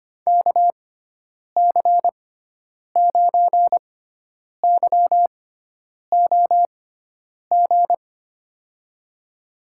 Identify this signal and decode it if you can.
Morse code, heard as a keyed tone, about 25 wpm, spaced as for 9 wpm, 700 Hz. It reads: KC9YOG
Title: USA Call Signs - 6 Characters - 6x Character Spacing - 25wpm